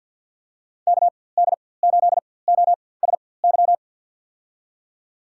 Morse code, heard as a keyed tone, about 53 wpm, spaced as for 24 wpm, 700 Hz. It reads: KD8OSY
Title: USA Call Signs - 6 Characters - 4x Character Spacing - 50wpm